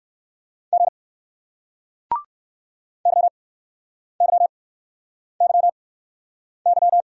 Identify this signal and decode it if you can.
Morse code, heard as a keyed tone, about 60 wpm, spaced as for 9 wpm, 700 Hz. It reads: G YYYY